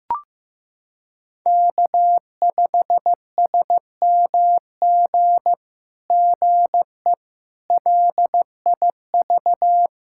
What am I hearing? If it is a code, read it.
Morse code, heard as a keyed tone, about 15 wpm, 700 Hz. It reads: K5SMG GE LIV
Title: SOTA QSO Practice 15wpm